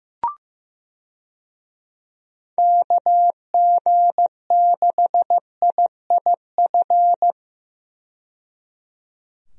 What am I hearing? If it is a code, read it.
Morse code, heard as a keyed tone, about 15 wpm, 700 Hz. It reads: KG6IIF
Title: USA Call Signs - 6 Characters - 15wpm